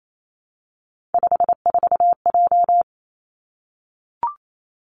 Morse code, heard as a keyed tone, about 28 wpm, 700 Hz. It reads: 54J
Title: Three Challenging Letter-Number Combinations 28wpm